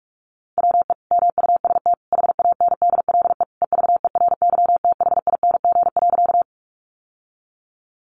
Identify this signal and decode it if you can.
Morse code, heard as a keyed tone, about 44 wpm, 700 Hz. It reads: WE MUST HANDLE EVERYTHING.
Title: Sentences from Top 1100 Words 45wpm